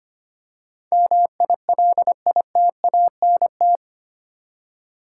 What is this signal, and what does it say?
Morse code, heard as a keyed tone, about 25 wpm, 700 Hz. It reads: MILITANT
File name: Eight Letter Words - 25wpm